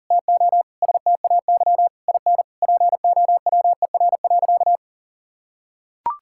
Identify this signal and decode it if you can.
Morse code, heard as a keyed tone, about 40 wpm, 700 Hz. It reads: TO STAY IN POWER.
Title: Sentences from Top 1000 Words - Review Part 5 - 40wpm